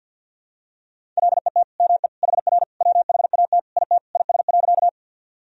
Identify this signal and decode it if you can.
Morse code, heard as a keyed tone, about 50 wpm, 700 Hz. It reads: LET ME HR WHAT IT IS.
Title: Sentences from Top 300 Words and Abbreviations 50wpm